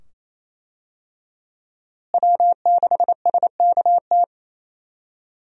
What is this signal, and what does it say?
Morse code, heard as a keyed tone, about 28 wpm, 700 Hz. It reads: W6SKT
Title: USA Call Signs - 5 Characters - Repeated 2x - 28wpm